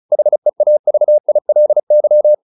Morse code, heard as a keyed tone, about 34 wpm, 600 Hz. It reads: HEAVILY
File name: LY7 35